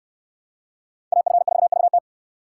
Morse code, heard as a keyed tone, about 59 wpm, 690 Hz. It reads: AF4VT